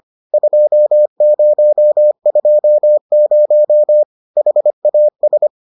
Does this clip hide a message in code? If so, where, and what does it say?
Yes: Morse code from 0.3 to 5.5 s, heard as a keyed tone, about 25 wpm, 600 Hz. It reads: 2020 HAS